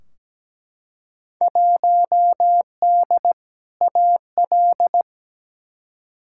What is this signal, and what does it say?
Morse code, heard as a keyed tone, about 17 wpm, 700 Hz. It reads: 1D AL
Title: Field Day Exchanges - 17wpm